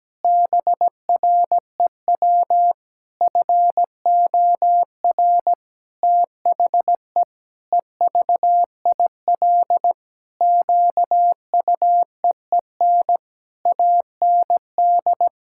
Morse code, heard as a keyed tone, about 17 wpm, 700 Hz. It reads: BREW FOR THE EVIL QUEEN AND
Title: Pangram Challenge v2 - 17wpm